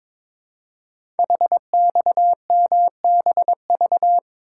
Morse code, heard as a keyed tone, about 23 wpm, 700 Hz. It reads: HXMBV
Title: Code Groups - 5 Letters - 22wpm